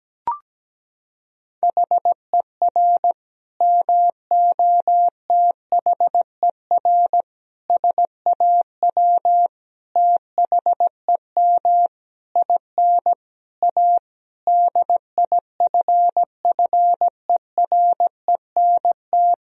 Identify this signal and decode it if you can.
Morse code, heard as a keyed tone, about 17 wpm, 700 Hz. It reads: HER MOTHER SAW THEM IN A DIFFERENT
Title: Sentences from Top 1000 Words - Review Part 3 - 17wpm